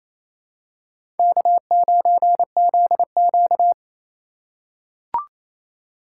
Morse code, heard as a keyed tone, about 28 wpm, 700 Hz. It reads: K9ZQ
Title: USA Call Signs - 4 Characters - 2x Character Spacing - 28wpm